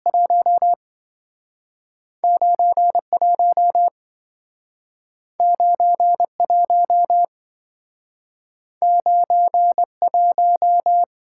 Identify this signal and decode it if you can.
Morse code, heard as a keyed tone, about 27 wpm, spaced as for 5 wpm, 700 Hz. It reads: 19 19 19 1